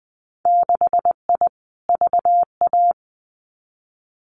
Morse code, heard as a keyed tone, about 20 wpm, 700 Hz. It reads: 6I VA
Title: Winter Field Day Exchanges Encore 20wpm